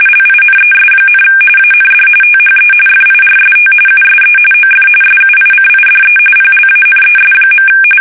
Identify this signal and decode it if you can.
RTTY, about 45.45 baud. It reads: LINES, AND THE FIRST USE OF SUICIDE PLANES.
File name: fsk850 / battleshipnj